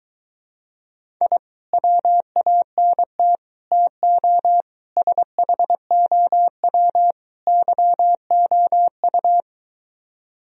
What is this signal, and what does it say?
Morse code, heard as a keyed tone, about 23 wpm, 700 Hz. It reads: I WANT TO SHOW YOU